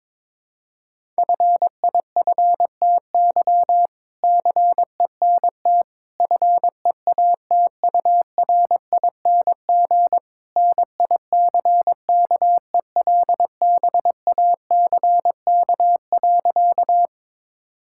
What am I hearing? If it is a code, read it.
Morse code, heard as a keyed tone, about 22 wpm, 700 Hz. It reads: FIFTY CENT FEATURING NICKELBACK.
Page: morse-code-ninja-practice-22wpm